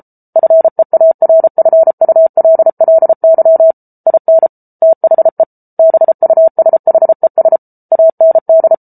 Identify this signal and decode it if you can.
Morse code, heard as a keyed tone, about 34 wpm, 650 Hz. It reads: FEARFULLY IN THE BUSHES AND